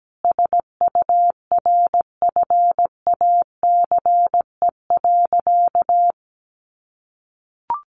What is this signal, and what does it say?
Morse code, heard as a keyed tone, about 17 wpm, 700 Hz. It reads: SURFACE.
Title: Sentences from Top 600 Words 17wpm